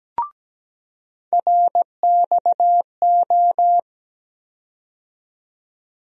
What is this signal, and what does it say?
Morse code, heard as a keyed tone, about 17 wpm, 700 Hz. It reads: RXO
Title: USA Call Sign Postfixes - 3 Letters 17wpm